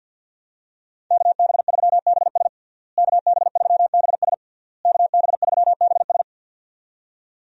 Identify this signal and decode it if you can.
Morse code, heard as a keyed tone, about 52 wpm, 700 Hz. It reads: KB3BS KB3BS KB3BS
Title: USA Call Signs - 5 Characters - Repeated 3x - 50wpm